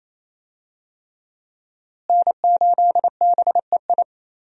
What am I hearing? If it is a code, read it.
Morse code, heard as a keyed tone, about 28 wpm, 700 Hz. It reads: N8BEI
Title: USA Call Signs - 5 Characters - Repeated 3x - 28wpm